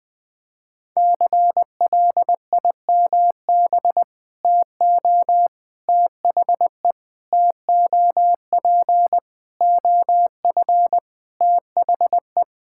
Morse code, heard as a keyed tone, about 21 wpm, 700 Hz. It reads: CLIMB TO THE TOP OF THE
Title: Sentences with 4-Word Phrases - Spoken Beginning - 20wpm